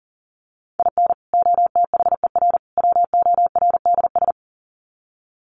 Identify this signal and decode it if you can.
Morse code, heard as a keyed tone, about 40 wpm, 700 Hz. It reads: IN OTHER WORDS
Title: Sets of 3 Words 40wpm